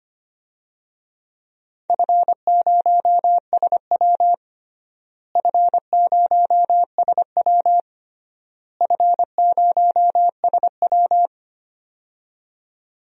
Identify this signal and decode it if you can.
Morse code, heard as a keyed tone, about 25 wpm, 700 Hz. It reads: F0SW F0SW F0SW
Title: DXCC-100 Call Signs - 4 Characters - Repeated 3x - 25wpm